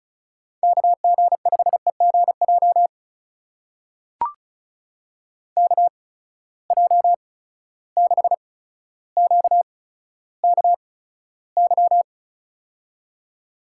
Morse code, heard as a keyed tone, about 36 wpm, spaced as for 15 wpm, 700 Hz. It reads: KG5EGJ KJ6QKY
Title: USA Call Signs - 6 Characters - 8x Character Spacing - 35wpm